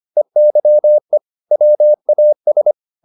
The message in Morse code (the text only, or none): EYE WAS